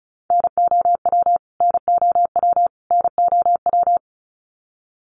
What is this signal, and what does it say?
Morse code, heard as a keyed tone, about 34 wpm, 700 Hz. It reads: NOW NOW NOW
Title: Top 300 Words Repeated 35wpm